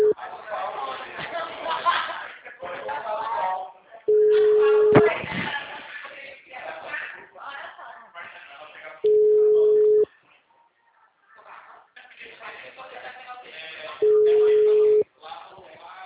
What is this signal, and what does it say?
Ringback tone (Europe, 425 Hz), with speech.